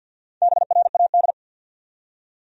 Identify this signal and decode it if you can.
Morse code, heard as a keyed tone, about 51 wpm, 700 Hz. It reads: BRAD
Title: USA Common Names - Max 4 Letters - 50wpm